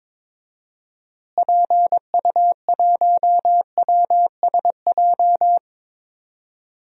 Morse code, heard as a keyed tone, about 22 wpm, 700 Hz. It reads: PU1WSJ
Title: DXCC-100 Call Signs - 6 Characters - Repeated 3x - 22wpm